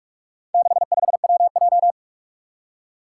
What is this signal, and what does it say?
Morse code, heard as a keyed tone, about 44 wpm, 700 Hz. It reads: 65WJ
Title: Four Challenging Letter-Number Combinations 45wpm